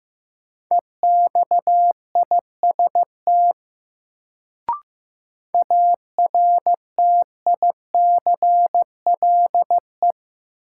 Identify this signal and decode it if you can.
Morse code, heard as a keyed tone, about 15 wpm, 700 Hz. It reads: EXIST ARTICLE
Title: Sentences from Top 1000 Words - Review Part 5 - New Word Drill - 15wpm